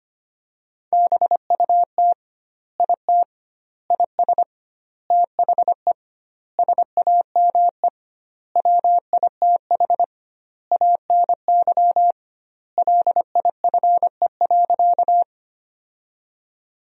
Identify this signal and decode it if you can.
Morse code, heard as a keyed tone, about 25 wpm, 700 Hz. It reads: BUT IT IS THE SAME WITH ANY LIFE.
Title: 2x Word-Spacing - Sentences from Top 300 Words - 25wpm